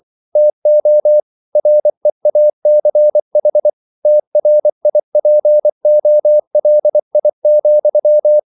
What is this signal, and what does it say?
Morse code, heard as a keyed tone, about 24 wpm, 600 Hz. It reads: TO REACH TRIPOLI,